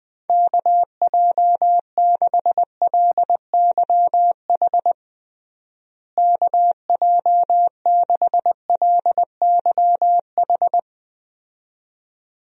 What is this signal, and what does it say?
Morse code, heard as a keyed tone, about 20 wpm, 700 Hz. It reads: KJ6LYH KJ6LYH
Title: USA Call Signs - 6 Characters - Repeated 2x - 20wpm